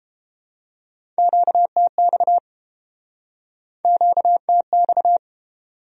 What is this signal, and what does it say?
Morse code, heard as a keyed tone, about 32 wpm, 700 Hz. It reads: QTX QTX